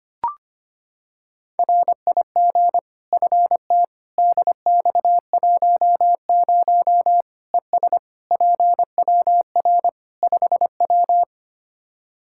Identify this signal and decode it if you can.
Morse code, heard as a keyed tone, about 25 wpm, 700 Hz. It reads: RIG FT DX10 ES PWR 5W
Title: Practice QSO - Exchange 2 - Info - 25wpm